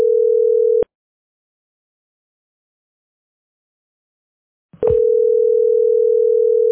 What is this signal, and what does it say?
Ringback tone (North America), 440 + 480 Hz.